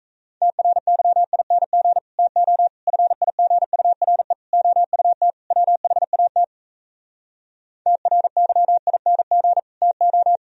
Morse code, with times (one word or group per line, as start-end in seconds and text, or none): 0.4-2.0 TRYING
2.2-2.7 TO
2.9-4.3 FIGURE
4.5-5.3 OUT
5.5-6.4 WHAT
7.9-9.6 TRYING
9.8-10.4 TO